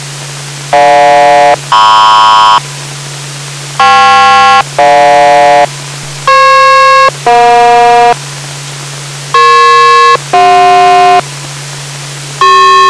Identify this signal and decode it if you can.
Aircraft SELCAL, heard as two pairs of tones. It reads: GJ-MP KP-GJ FP-EJ EP-CJ